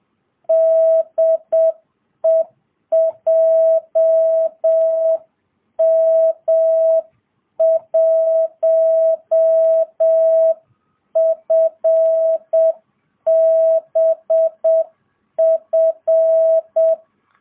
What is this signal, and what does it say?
Morse code, heard as a keyed tone, about 7 wpm, 660 Hz. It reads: DE JM1FBF